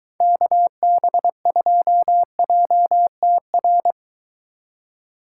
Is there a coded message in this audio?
Yes: Morse code from 0.2 to 3.9 s, heard as a keyed tone, about 23 wpm, 700 Hz. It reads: KB2JTR